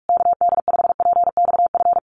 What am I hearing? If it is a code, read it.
Morse code, heard as a keyed tone, about 44 wpm, 700 Hz. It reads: KD5PXF
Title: USA Call Signs Repeated 3x 45wpm